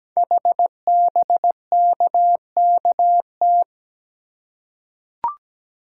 Morse code, heard as a keyed tone, about 17 wpm, 700 Hz. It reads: HBKKT
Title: Code Groups - 5 Letters - 17wpm